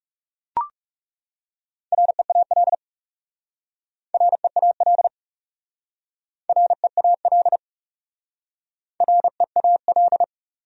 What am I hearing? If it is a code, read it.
Morse code, heard as a keyed tone, about 37 wpm, 700 Hz. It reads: REAL REAL REAL REAL
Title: Speed-Racing Top 500 Words 30wpm